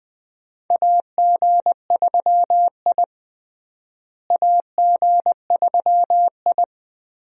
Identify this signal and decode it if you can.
Morse code, heard as a keyed tone, about 20 wpm, 700 Hz. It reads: AG3I AG3I
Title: USA Call Signs - 4 Characters - Repeated 2x - 20wpm